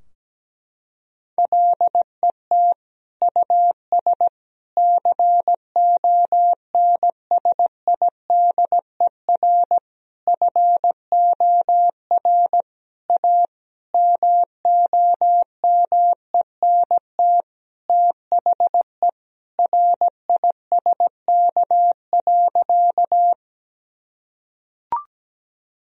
Morse code, heard as a keyed tone, about 17 wpm, 700 Hz. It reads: LET US CONSIDER FOR A MOMENT THE RISK.
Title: Sentences from Top 1000 Words - Review Part 4 - 17wpm